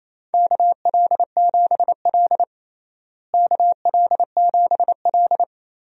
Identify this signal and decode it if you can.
Morse code, heard as a keyed tone, about 28 wpm, 700 Hz. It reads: KL7L KL7L
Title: DXCC-100 Call Signs - 4 Characters - Repeated 3x - 28wpm